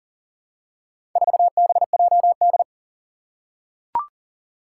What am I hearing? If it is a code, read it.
Morse code, heard as a keyed tone, about 39 wpm, 700 Hz. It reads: 4BJD
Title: Four Challenging Letter-Number Combinations 40wpm